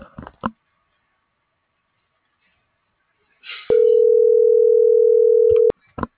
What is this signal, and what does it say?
Ringback tone (North America), 440 + 480 Hz.